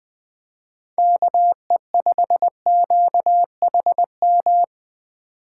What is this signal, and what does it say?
Morse code, heard as a keyed tone, about 23 wpm, 700 Hz. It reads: KE5QHM
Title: Speed Racing - USA 2x3 Call Signs 20wpm